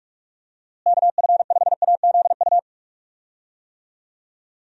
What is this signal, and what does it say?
Morse code, heard as a keyed tone, about 47 wpm, 700 Hz. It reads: KF5AZU
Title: USA Call Signs - 6 Characters - Repeated 2x - 45wpm